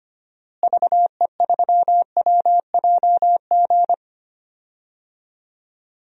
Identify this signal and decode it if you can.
Morse code, heard as a keyed tone, about 25 wpm, 700 Hz. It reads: VE3WJG
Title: DXCC-100 Call Signs - 6 Characters - Repeated 2x - 25wpm